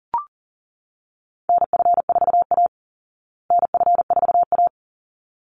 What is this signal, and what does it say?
Morse code, heard as a keyed tone, about 40 wpm, 700 Hz. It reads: NF4A NF4A
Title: Call Signs Repeated 3x - 40wpm